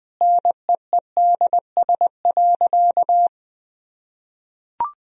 Morse code, heard as a keyed tone, about 21 wpm, 700 Hz. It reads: NEEDS.
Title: Sentences from Top 1000 Words - Review Part 3 - 20wpm